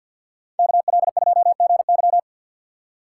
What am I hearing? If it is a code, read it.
Morse code, heard as a keyed tone, about 64 wpm, 700 Hz. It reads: KL2GY